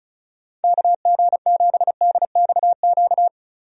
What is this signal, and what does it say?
Morse code, heard as a keyed tone, about 35 wpm, 700 Hz. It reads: KG7DXQ
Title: USA Call Signs - 6 Characters - Repeated 2x - 35wpm